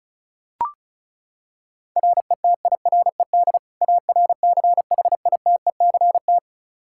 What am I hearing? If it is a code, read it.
Morse code, heard as a keyed tone, about 36 wpm, 700 Hz. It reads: RETIRED ARCHITECT
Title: Practice QSO - Exchange 3 - More Info - 35wpm